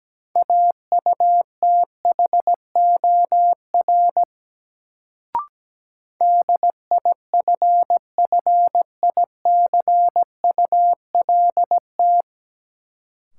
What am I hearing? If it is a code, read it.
Morse code, heard as a keyed tone, about 17 wpm, 700 Hz. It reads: AUTHOR DIFFICULT
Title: Sets of 4 Words - New Word Drill - 17wpm